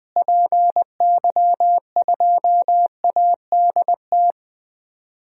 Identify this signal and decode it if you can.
Morse code, heard as a keyed tone, about 21 wpm, 700 Hz. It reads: PY2ADT
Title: DXCC-100 Call Signs - 6 Characters - Repeated 2x - 20wpm